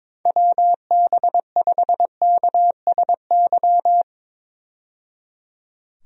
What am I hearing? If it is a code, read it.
Morse code, heard as a keyed tone, about 22 wpm, 700 Hz. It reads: WB5KSY
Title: USA Call Signs - 6 Characters - 22wpm